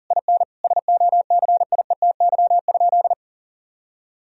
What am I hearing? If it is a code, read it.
Morse code, heard as a keyed tone, about 40 wpm, 700 Hz. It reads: IN SOCIETY?
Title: Sentences from Top 1000 Words - Review Part 4 - 40wpm